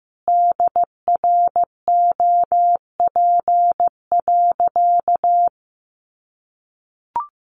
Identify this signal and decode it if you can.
Morse code, heard as a keyed tone, about 15 wpm, 700 Hz. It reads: DROP.
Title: Sentences from Top 1400 Words 15wpm